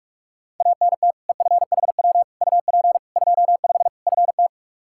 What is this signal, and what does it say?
Morse code, heard as a keyed tone, about 41 wpm, 700 Hz. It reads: ANT EFHW UP 25 FT
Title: K1USN SST QSO Practice 45wpm